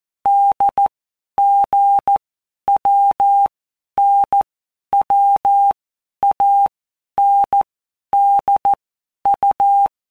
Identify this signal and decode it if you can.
Morse code, heard as a keyed tone, about 14 wpm, 800 Hz. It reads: D G W N W A N D U
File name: Basic Letter Groups By Blocks / Groups@13 / AWUNDG@13